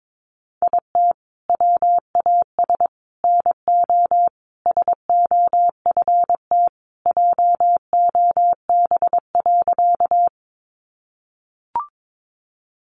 Morse code, heard as a keyed tone, about 22 wpm, 700 Hz. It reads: IT WAS NO SOFT JOB.
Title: Sentences from Top 1500 Words 22wpm